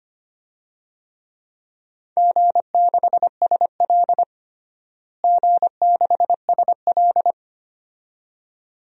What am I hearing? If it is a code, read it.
Morse code, heard as a keyed tone, about 25 wpm, 700 Hz. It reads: G6SL G6SL
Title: DXCC-100 Call Signs - 4 Characters - Repeated 2x - 25wpm